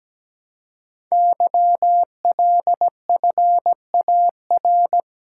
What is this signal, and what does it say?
Morse code, heard as a keyed tone, about 17 wpm, 700 Hz. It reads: YLFAR
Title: Code Groups - 5 Letters - 17wpm